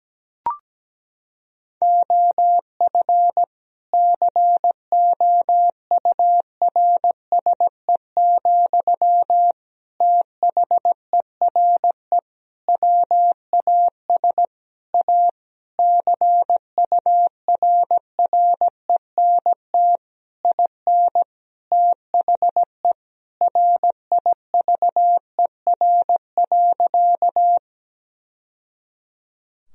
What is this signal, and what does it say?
Morse code, heard as a keyed tone, about 17 wpm, 700 Hz. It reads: OF COURSE, THERE WAS A CURRENT IN THE RIVER.
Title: Sentences from Top 1000 Words - Review Part 4 - 17wpm